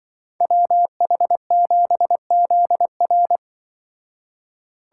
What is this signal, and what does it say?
Morse code, heard as a keyed tone, about 24 wpm, 700 Hz. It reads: WH7ZR
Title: Speed Racing - USA 2x2 Call Signs 20wpm